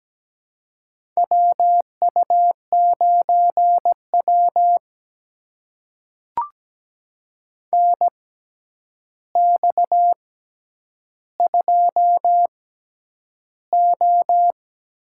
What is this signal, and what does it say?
Morse code, heard as a keyed tone, about 17 wpm, spaced as for 9 wpm, 700 Hz. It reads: WU9W NX2O
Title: USA Call Signs - 4 Characters - 6x Character Spacing - 17wpm